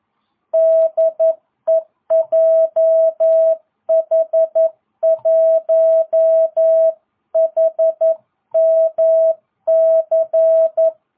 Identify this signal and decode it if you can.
Morse code, heard as a keyed tone, about 11 wpm, 650 Hz. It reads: DE JH1HMC